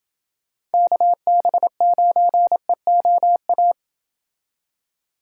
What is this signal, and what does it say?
Morse code, heard as a keyed tone, about 29 wpm, 700 Hz. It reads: KB9EOA